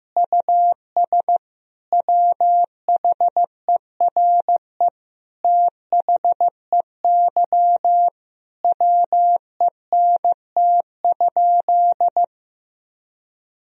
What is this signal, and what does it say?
Morse code, heard as a keyed tone, about 15 wpm, 700 Hz. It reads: US WHERE THEY WENT?
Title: Sentences from Top 1000 Words - Review Part 3 - 15wpm